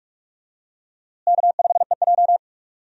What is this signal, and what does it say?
Morse code, heard as a keyed tone, about 43 wpm, 700 Hz. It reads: K5EJ
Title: USA Call Signs - 4 Characters - Repeated 3x - 45wpm